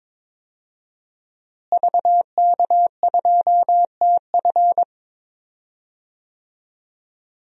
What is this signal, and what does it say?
Morse code, heard as a keyed tone, about 22 wpm, 700 Hz. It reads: VK2TF
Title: DXCC-100 Call Signs - 5 Characters - Repeated 3x - 22wpm